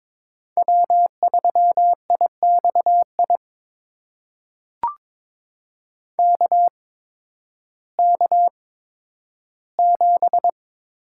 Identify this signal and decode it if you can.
Morse code, heard as a keyed tone, about 22 wpm, spaced as for 12 wpm, 700 Hz. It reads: W3IXI KK7
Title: USA Call Signs - 5 Characters - 8x Character Spacing - 22wpm